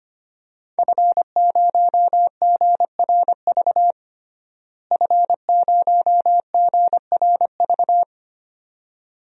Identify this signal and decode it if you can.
Morse code, heard as a keyed tone, about 25 wpm, 700 Hz. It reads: F0GRV F0GRV
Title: DXCC-100 Call Signs - 5 Characters - Repeated 2x - 25wpm